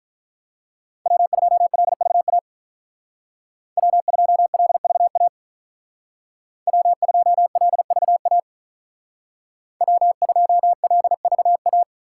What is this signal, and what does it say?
Morse code, heard as a keyed tone, about 46 wpm, 700 Hz. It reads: W2LVA W2LVA W2LVA W2LVA